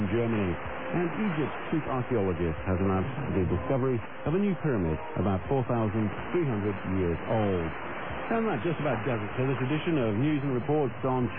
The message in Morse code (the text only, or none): none